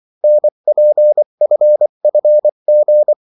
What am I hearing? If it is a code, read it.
Morse code, heard as a keyed tone, about 24 wpm, 600 Hz. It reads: NPFFG
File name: morse mp3 / revision9-vitesse 22 mots